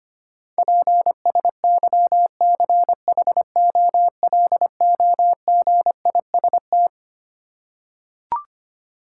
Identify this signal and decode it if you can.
Morse code, heard as a keyed tone, about 25 wpm, 700 Hz. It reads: PSYCHOLOGIST
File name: Twelve Letter Words - 25wpm